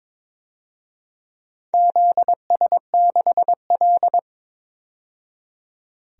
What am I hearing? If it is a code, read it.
Morse code, heard as a keyed tone, about 22 wpm, 700 Hz. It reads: ZS6L
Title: DXCC-100 Call Signs - 4 Characters - 22wpm